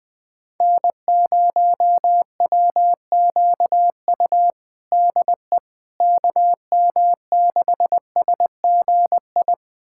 Morse code, heard as a keyed tone, about 21 wpm, 700 Hz. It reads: N0WQU DE KM6SGI
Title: USA Callsign DE Callsign 20wpm